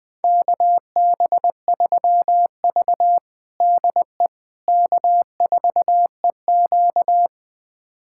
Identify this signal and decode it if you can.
Morse code, heard as a keyed tone, about 21 wpm, 700 Hz. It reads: KB3V DE K4EQ
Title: USA Callsign DE Callsign - 4 Characters - 20wpm